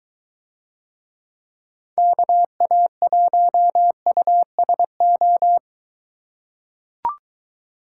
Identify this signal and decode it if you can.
Morse code, heard as a keyed tone, about 23 wpm, 700 Hz. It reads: KA1USO